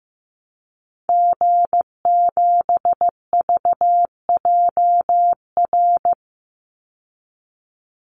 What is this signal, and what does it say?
Morse code, heard as a keyed tone, about 15 wpm, 700 Hz. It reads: G7VJR
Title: Call Signs Repeated 2x - 15wpm